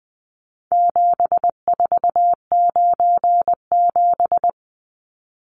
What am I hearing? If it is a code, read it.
Morse code, heard as a keyed tone, about 20 wpm, 700 Hz. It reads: 7497
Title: Numbers - 4-Digit 20wpm